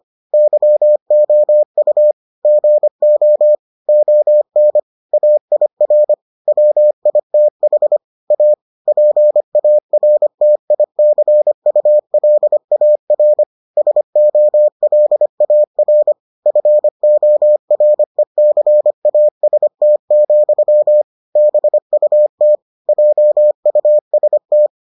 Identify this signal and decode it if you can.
Morse code, heard as a keyed tone, about 25 wpm, 600 Hz. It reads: YOU GO ON AIR WITH A PARTICULAR SOLAR FORECAST, BUT JUST